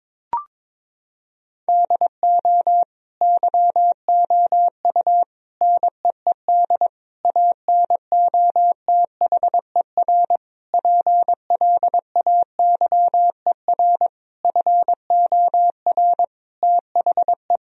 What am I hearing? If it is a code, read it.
Morse code, heard as a keyed tone, about 22 wpm, 700 Hz. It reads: DO YOU NEED ANOTHER PLAYER FOR THE
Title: Sentences from Top 1000 Words - Review Part 4 - 22wpm